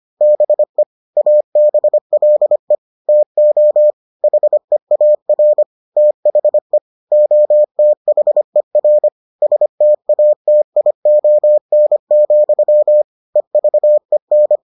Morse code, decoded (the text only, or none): BE ABLE TO HEAR THE OTHER STATION, EVEN